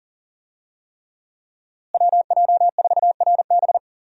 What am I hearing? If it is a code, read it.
Morse code, heard as a keyed tone, about 40 wpm, 700 Hz. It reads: WJ4RB
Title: USA Call Signs - 5 Characters - Repeated 3x - 40wpm